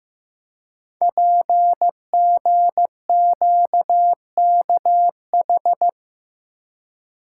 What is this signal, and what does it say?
Morse code, heard as a keyed tone, about 15 wpm, 700 Hz. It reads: PGQKH
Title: Code Groups - 5 Letters - 15wpm